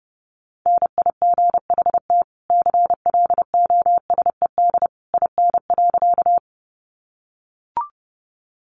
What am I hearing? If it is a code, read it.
Morse code, heard as a keyed tone, about 30 wpm, 700 Hz. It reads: NIGHT CLOSED IN.
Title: Sentences from Top 1400 Words 30wpm